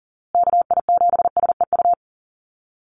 Morse code, heard as a keyed tone, about 40 wpm, 700 Hz. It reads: KI7SEU
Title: USA Call Signs Repeated 3x 40wpm